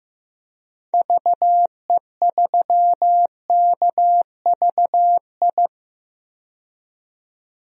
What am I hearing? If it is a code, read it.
Morse code, heard as a keyed tone, about 15 wpm, 700 Hz. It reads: VE3KVI
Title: DXCC-100 Call Signs - 6 Characters - 15wpm